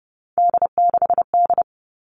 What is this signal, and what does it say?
Morse code, heard as a keyed tone, about 30 wpm, 700 Hz. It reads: D6D
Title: All Cut Numbers - 3-Digit 30wpm